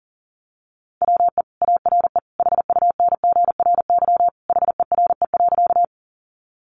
Morse code, heard as a keyed tone, about 40 wpm, 700 Hz. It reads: WE ARE HUNGRY HERE.
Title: Sentences from Top 1400 Words 40wpm